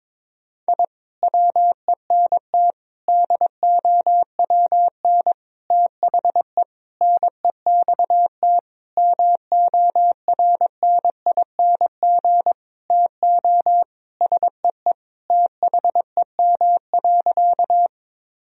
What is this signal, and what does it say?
Morse code, heard as a keyed tone, about 21 wpm, 700 Hz. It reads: I WENT DOWN THE NEXT MORNING TO SEE THEM.
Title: Sentences from Top 1000 Words - Review Part 1 - 22wpm